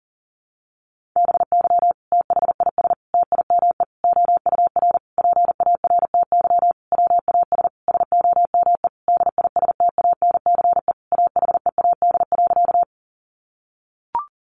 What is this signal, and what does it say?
Morse code, heard as a keyed tone, about 40 wpm, 700 Hz. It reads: BY THIS TIME OUR PARTY WAS SOME DISTANCE AHEAD.
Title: Sentences from Top 1400 Words 40wpm